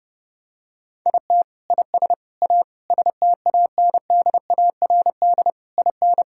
Morse code, heard as a keyed tone, about 32 wpm, 700 Hz. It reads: IT IS A STANDARD IN
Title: Sentences from Top 1000 Words - Review Part 6 - 30wpm